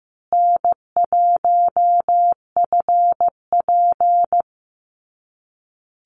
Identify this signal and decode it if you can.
Morse code, heard as a keyed tone, about 15 wpm, 700 Hz. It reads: N1FP
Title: USA Call Signs Repeated 3x 15wpm